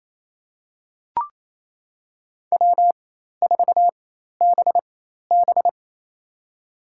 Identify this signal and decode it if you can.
Morse code, heard as a keyed tone, about 28 wpm, spaced as for 13 wpm, 700 Hz. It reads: W4BB
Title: USA Call Signs - 4 Characters - 4x Character Spacing - 28wpm